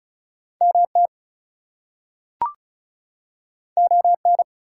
Morse code, heard as a keyed tone, about 36 wpm, 700 Hz. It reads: MT ON
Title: US States and Canadian Provinces 35wpm